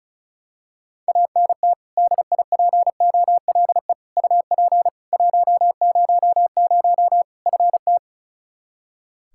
Morse code, heard as a keyed tone, about 34 wpm, 700 Hz. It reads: ANT DIPOLE UP 100 FT
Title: Practice QSO - Exchange 2 - Info - 35wpm